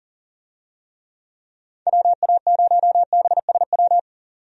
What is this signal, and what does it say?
Morse code, heard as a keyed tone, about 40 wpm, 700 Hz. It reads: WA0BSW
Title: USA Call Signs - 6 Characters - 40wpm